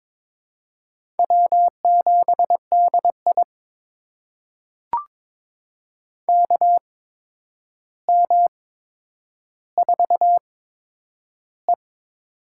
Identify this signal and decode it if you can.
Morse code, heard as a keyed tone, about 22 wpm, spaced as for 9 wpm, 700 Hz. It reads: W7DI KM4E
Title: USA Call Signs - 4 Characters - 8x Character Spacing - 22wpm